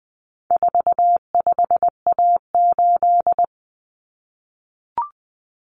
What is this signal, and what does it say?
Morse code, heard as a keyed tone, about 20 wpm, 700 Hz. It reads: 45A8
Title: Common Cut Numbers - 4-Digit 20wpm